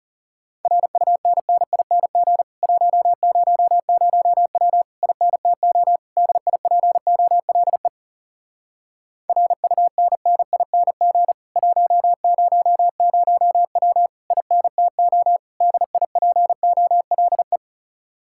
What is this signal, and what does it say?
Morse code, heard as a keyed tone, about 40 wpm, 700 Hz. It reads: RUNNING 100W INTO DIPOLE RUNNING 100W INTO DIPOLE